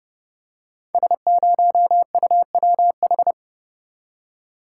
Morse code, heard as a keyed tone, about 30 wpm, 700 Hz. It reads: S0UWH
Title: Code Groups - 5 Letter:Numbers - 30wpm